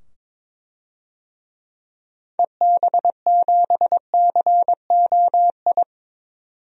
Morse code, heard as a keyed tone, about 22 wpm, 700 Hz. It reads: EB7COI
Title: DXCC-100 Call Signs - 6 Characters - 22wpm